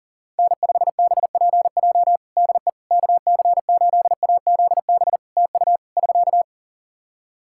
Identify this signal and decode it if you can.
Morse code, heard as a keyed tone, about 37 wpm, 700 Hz. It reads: N5BPJ DE KC8AZB TU <SK>
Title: K1USN SST QSO Practice 40wpm